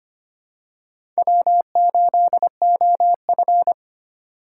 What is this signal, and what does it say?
Morse code, heard as a keyed tone, about 25 wpm, 700 Hz. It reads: W8OF
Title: USA Call Signs - 4 Characters - 3x Character Spacing - 25wpm